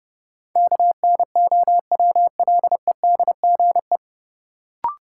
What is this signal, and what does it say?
Morse code, heard as a keyed tone, about 30 wpm, 700 Hz. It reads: KNOWLEDGE
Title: Sentences from Top 600 Words - New Word Drill - 30wpm